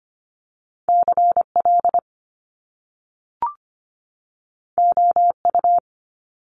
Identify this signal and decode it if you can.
Morse code, heard as a keyed tone, about 25 wpm, 700 Hz. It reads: CL OU
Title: Top Two Letter Combinations in Words 25wpm